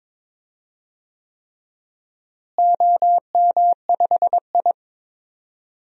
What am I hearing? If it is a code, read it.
Morse code, heard as a keyed tone, about 22 wpm, 700 Hz. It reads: OM5I
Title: DXCC-100 Call Signs - 4 Characters - Repeated 3x - 22wpm